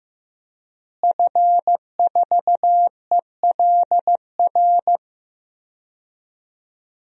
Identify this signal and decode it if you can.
Morse code, heard as a keyed tone, about 15 wpm, 700 Hz. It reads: F4ELR
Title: DXCC-100 Call Signs - 5 Characters - Repeated 3x - 15wpm